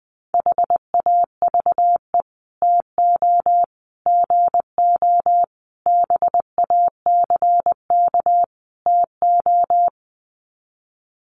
Morse code, heard as a keyed tone, about 20 wpm, 700 Hz. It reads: HAVE TO GO BACK TO
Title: Sets of 5 Words Encore2 20wpm